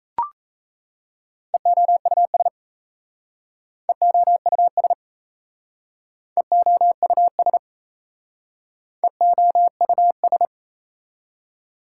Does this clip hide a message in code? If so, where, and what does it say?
Yes: Morse code from 1.5 to 10.5 s, heard as a keyed tone, about 36 wpm, 690 Hz. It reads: EOUS EOUS EOUS EOUS